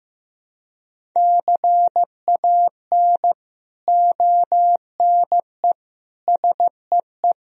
Morse code, heard as a keyed tone, about 16 wpm, 700 Hz. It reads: CAN ONE SEE
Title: Sentences from Top 100 Words and Abbreviations 15wpm